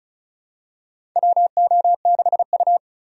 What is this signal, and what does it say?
Morse code, heard as a keyed tone, about 36 wpm, 700 Hz. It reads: WO6U
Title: USA Call Signs - 4 Characters - Repeated 3x - 35wpm